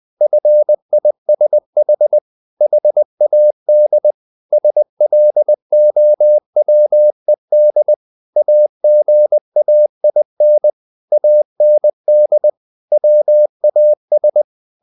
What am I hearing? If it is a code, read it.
Morse code, heard as a keyed tone, about 20 wpm, 600 Hz. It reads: FISH HAD SLOWED AGAIN AND WAS
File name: the old man and the sea / the old man and the sea part 1 20wpm